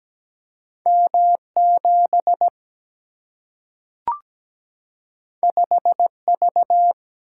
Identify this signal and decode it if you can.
Morse code, heard as a keyed tone, about 17 wpm, 700 Hz. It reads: M7 5V
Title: Single-Letter Single-Number 17wpm